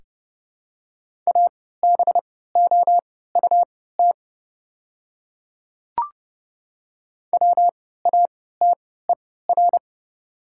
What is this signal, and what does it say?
Morse code, heard as a keyed tone, about 31 wpm, spaced as for 17 wpm, 700 Hz. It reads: ABOUT WATER
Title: 3x Character Spacing - Top 100 Words - 30wpm